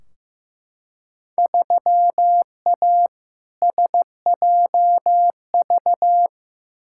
Morse code, heard as a keyed tone, about 15 wpm, 700 Hz. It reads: 3A SJV
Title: Field Day Exchanges Encore - 15wpm